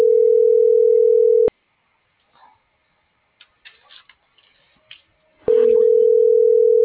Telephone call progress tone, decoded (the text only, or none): ringback tone